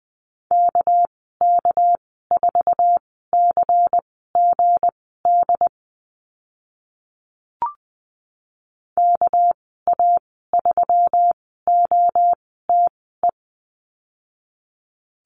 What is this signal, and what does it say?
Morse code, heard as a keyed tone, about 20 wpm, spaced as for 14 wpm, 700 Hz. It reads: KK4CGD KA3OTE
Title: 213 Lesson 34g - Number 8 - Call Signs 20wpm